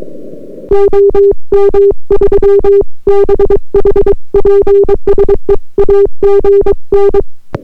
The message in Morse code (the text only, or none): OM3BH PSE AGN